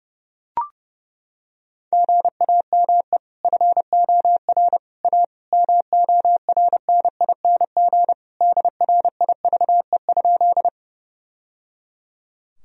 Morse code, heard as a keyed tone, about 30 wpm, 700 Hz. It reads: GAME FOR A MORNING DRIVE?
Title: Sentences from Top 1000 Words - Review Part 3 - 30wpm